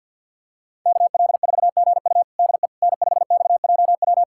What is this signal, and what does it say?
Morse code, heard as a keyed tone, about 50 wpm, 700 Hz. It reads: KL4GU DE N5XJW
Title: USA Callsign DE Callsign - 5 Characters - 50wpm